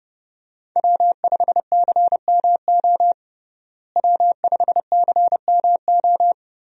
Morse code, heard as a keyed tone, about 31 wpm, 700 Hz. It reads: W5CMO W5CMO
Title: USA Call Signs - 5 Characters - Repeated 2x - 30wpm